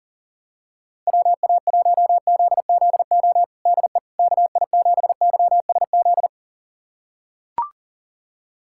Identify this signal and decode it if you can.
Morse code, heard as a keyed tone, about 40 wpm, 700 Hz. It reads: WA1ZZO DE KI7YSZ
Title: USA Callsign DE Callsign - 6 Characters - 40wpm